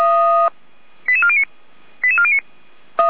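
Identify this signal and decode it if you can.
Five-tone selective call (ZVEI), heard as quick runs of five tones, tones of about 70 ms. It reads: 88409 88409
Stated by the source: ZVEI five-tone call received on SDR by feuerwehr-orlamuende fire brigade.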